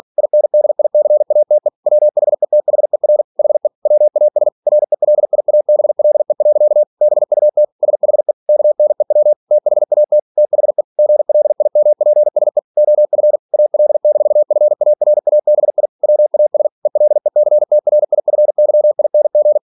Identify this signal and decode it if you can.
Morse code, heard as a keyed tone, about 46 wpm, 600 Hz. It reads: INDICATE WHETHER HE WAS RELIABLE. BUT SHE KNEW THAT THE GLIMPSE OF AL-FARABI WAS ELECTRIFYING
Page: Never 47wpm